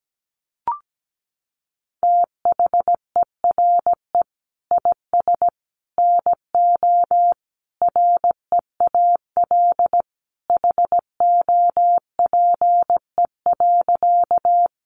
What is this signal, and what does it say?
Morse code, heard as a keyed tone, about 17 wpm, 700 Hz. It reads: THERE IS NO REAL HOPE.
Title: Sentences from Top 1000 Words 17wpm